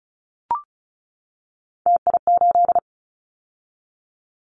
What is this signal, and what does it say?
Morse code, heard as a keyed tone, about 34 wpm, 700 Hz. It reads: TI8
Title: Callsign Prefixes for Top 100 DXCC - 35wpm